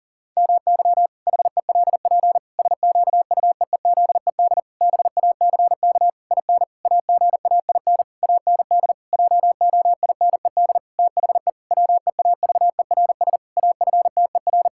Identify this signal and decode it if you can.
Morse code, heard as a keyed tone, about 40 wpm, 700 Hz. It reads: MY HELP SQUEEZED BACK IN AGAIN AND JOINED THE WEAVERS AFTER
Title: Pangram Challenge v2 - 40wpm